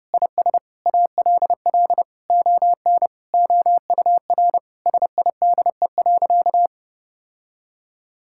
Morse code, heard as a keyed tone, about 30 wpm, 700 Hz. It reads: IS ALL ON OUR SIDE.
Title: Sentences from Top 1000 Words - Review Part 6 - 30wpm